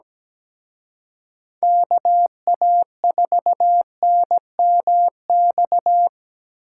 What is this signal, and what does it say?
Morse code, heard as a keyed tone, about 20 wpm, 700 Hz. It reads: KA4NMX
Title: Speed Racing - USA Call Signs 17wpm